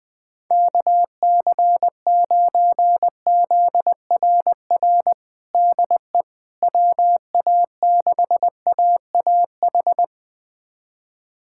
Morse code, heard as a keyed tone, about 20 wpm, 700 Hz. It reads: KC9ZRR DE WA6AAH
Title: USA Callsign DE Callsign 20wpm